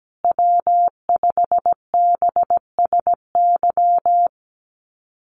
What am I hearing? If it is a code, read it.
Morse code, heard as a keyed tone, about 17 wpm, 700 Hz. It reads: W5BSY
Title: USA Call Signs Repeated 2x 17wpm